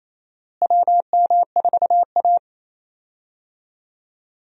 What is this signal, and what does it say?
Morse code, heard as a keyed tone, about 28 wpm, 700 Hz. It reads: WM4A
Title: USA Call Signs - 4 Characters - Repeated 3x - 28wpm